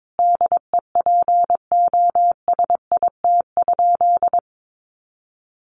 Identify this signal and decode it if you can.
Morse code, heard as a keyed tone, about 22 wpm, 700 Hz. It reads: DEPOSIT?
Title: Sentences from Top 1300 Words 22wpm